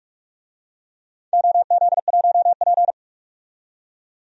Morse code, heard as a keyed tone, about 44 wpm, 700 Hz. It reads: OZ1P
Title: DXCC-100 Call Signs - 4 Characters - 45wpm